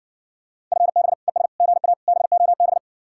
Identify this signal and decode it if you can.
Morse code, heard as a keyed tone, about 67 wpm, 700 Hz. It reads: FB ES GA BOB